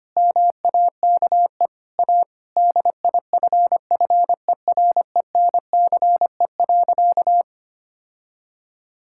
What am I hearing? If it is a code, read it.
Morse code, heard as a keyed tone, about 25 wpm, 700 Hz. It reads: MAKE A DIFFERENCE.
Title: Sentences from Top 1000 Words - Review Part 5 - 25wpm